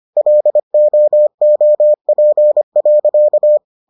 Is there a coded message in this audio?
Yes: Morse code from 0.2 to 3.6 s, heard as a keyed tone, about 25 wpm, 600 Hz. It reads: LOOP.